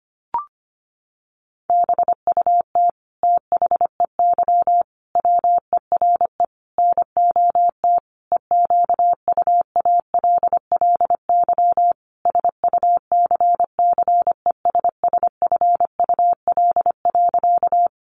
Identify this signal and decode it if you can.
Morse code, heard as a keyed tone, about 25 wpm, 700 Hz. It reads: BUT THEY WERE NOT EQUALLY SUCCESSFUL.
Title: Sentences from Top 1400 Words 25wpm